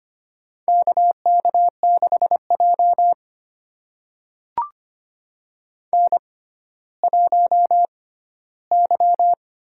Morse code, heard as a keyed tone, about 25 wpm, spaced as for 14 wpm, 700 Hz. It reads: KK6J N1Y
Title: USA Call Signs - 4 Characters - 6x Character Spacing - 25wpm